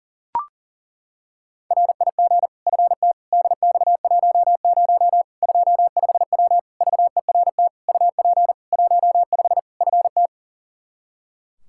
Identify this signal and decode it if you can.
Morse code, heard as a keyed tone, about 40 wpm, 700 Hz. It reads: RIG FT DX10 25W VERT UP 15 FT
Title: K1USN SST QSO Practice 40wpm